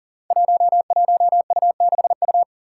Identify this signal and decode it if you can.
Morse code, heard as a keyed tone, about 40 wpm, 700 Hz. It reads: 11U6U
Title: Five Challenging Letter-Number Combinations 40wpm